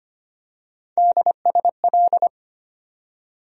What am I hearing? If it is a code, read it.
Morse code, heard as a keyed tone, about 25 wpm, 700 Hz. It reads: DSL
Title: USA Call Sign Postfixes - 3 Letters 25wpm